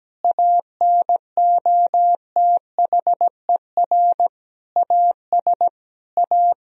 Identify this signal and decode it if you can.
Morse code, heard as a keyed tone, about 17 wpm, 700 Hz. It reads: ANOTHER AS A
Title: Sentences from Top 1000 Words - Review Part 1 - 17wpm